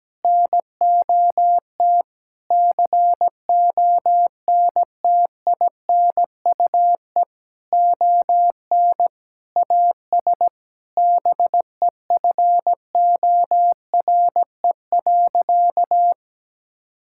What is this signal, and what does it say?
Morse code, heard as a keyed tone, about 17 wpm, 700 Hz. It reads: NOT CONTINUE ON AS BEFORE.
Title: Sentences from Top 1000 Words - Review Part 6 - 17wpm